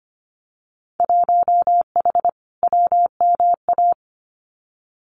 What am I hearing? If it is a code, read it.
Morse code, heard as a keyed tone, about 25 wpm, 700 Hz. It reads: 1H WMA
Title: Winter Field Day Exchanges 25wpm